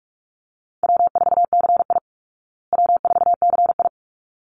Morse code, heard as a keyed tone, about 44 wpm, 700 Hz. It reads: W4CI W4CI
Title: Call Signs Repeated 3x - 45wpm